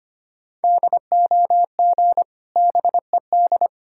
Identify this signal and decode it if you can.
Morse code, heard as a keyed tone, about 25 wpm, 700 Hz. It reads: DOG BED
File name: practice / 25wpm / Difficult Words with Hints - 25wpm